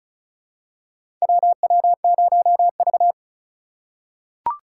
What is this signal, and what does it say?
Morse code, heard as a keyed tone, about 34 wpm, 700 Hz. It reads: WW0V
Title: USA Call Signs - 4 Characters - 4x Character Spacing - 35wpm